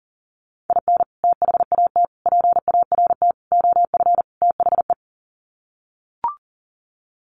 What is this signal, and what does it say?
Morse code, heard as a keyed tone, about 40 wpm, 700 Hz. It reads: IN THAT PART OF THE
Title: Sets of 5 Words Encore2 40wpm